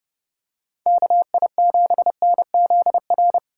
Morse code, heard as a keyed tone, about 30 wpm, 700 Hz. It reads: KI7NZR